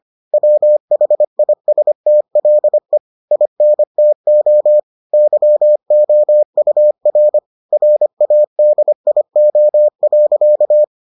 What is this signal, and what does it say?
Morse code, heard as a keyed tone, about 25 wpm, 600 Hz. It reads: WHISTLE INTO YOUR RADIO.